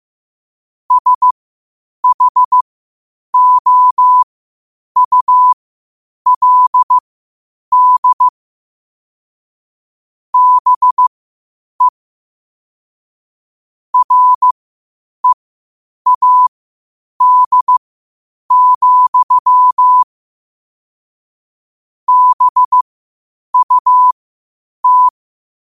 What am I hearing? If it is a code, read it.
Morse code, heard as a keyed tone, about 15 wpm, spaced as for 8 wpm, 1000 Hz. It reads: SHOULD BE READ, BUT
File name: qotdfiles / 8wpm